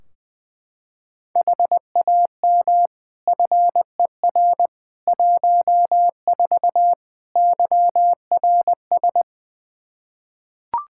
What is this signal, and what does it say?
Morse code, heard as a keyed tone, about 21 wpm, 700 Hz. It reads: HAM FER 14 YRS
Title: K1USN SST QSO Practice 20wpm